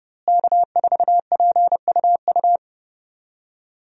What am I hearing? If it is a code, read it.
Morse code, heard as a keyed tone, about 30 wpm, 700 Hz. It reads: K4PUU
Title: USA Call Signs - 5 Characters - Repeated 3x - 30wpm